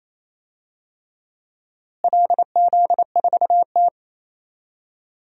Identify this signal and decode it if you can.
Morse code, heard as a keyed tone, about 28 wpm, 700 Hz. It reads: LZ4T
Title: DXCC-100 Call Signs - 4 Characters - Repeated 3x - 28wpm